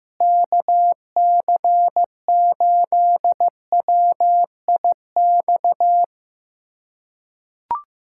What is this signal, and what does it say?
Morse code, heard as a keyed tone, about 15 wpm, 700 Hz. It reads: KC8WIX
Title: USA Callsign DE Callsign 15wpm